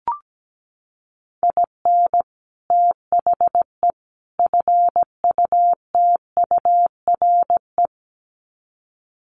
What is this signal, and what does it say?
Morse code, heard as a keyed tone, about 17 wpm, 700 Hz. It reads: IN THE FUTURE
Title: Sets of 3 Words 17wpm